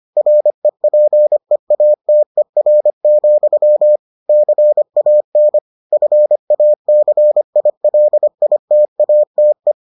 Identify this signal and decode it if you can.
Morse code, heard as a keyed tone, about 25 wpm, 600 Hz. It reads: REPEATER, CAN FACILITATE